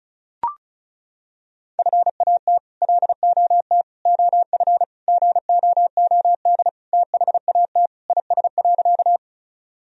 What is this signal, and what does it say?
Morse code, heard as a keyed tone, about 36 wpm, 700 Hz. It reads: FAT LOT OF GOOD THAT IS.
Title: Sentences from Top 1000 Words - Review Part 4 - 35wpm